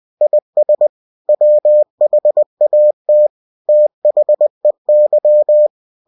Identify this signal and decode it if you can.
Morse code, heard as a keyed tone, about 20 wpm, 600 Hz. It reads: IS WHAT THEY
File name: cw-books / winnie-the-pooh / winnie-20wpm-page-0022